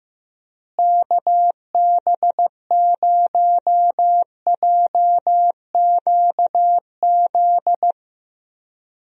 Morse code, heard as a keyed tone, about 15 wpm, 700 Hz. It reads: KB0JQZ
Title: USA Call Signs - 6 Characters - Repeated 3x - 15wpm